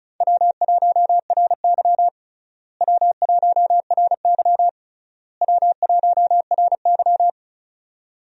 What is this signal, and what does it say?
Morse code, heard as a keyed tone, about 36 wpm, 700 Hz. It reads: W1RY W1RY W1RY
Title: USA Call Signs - 4 Characters - Repeated 3x - 35wpm